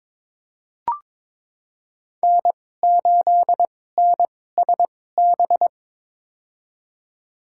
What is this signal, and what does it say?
Morse code, heard as a keyed tone, about 22 wpm, spaced as for 16 wpm, 700 Hz. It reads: N8NSB
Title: USA Call Signs - 5 Characters - 2x Character Spacing - 22wpm